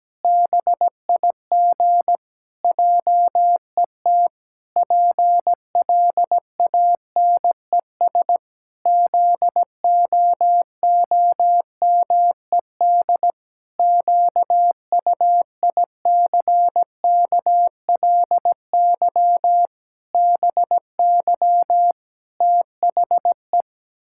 Morse code, heard as a keyed tone, about 17 wpm, 700 Hz. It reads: BIG JET PLANES ZOOMED QUICKLY BY THE
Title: Pangram Challenge v2 - 17wpm